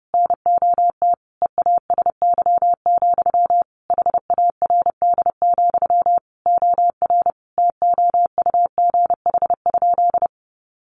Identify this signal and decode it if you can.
Morse code, heard as a keyed tone, about 30 wpm, 700 Hz. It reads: NOT EASY, HARD, OR TOUGH?
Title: Sentences from Top 1200 Words 30wpm